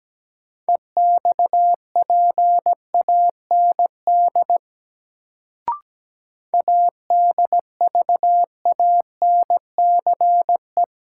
Morse code, heard as a keyed tone, about 17 wpm, 700 Hz. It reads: EXPAND ADVANCE
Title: Sentences from Top 1400 Words - New Word Drill - 17wpm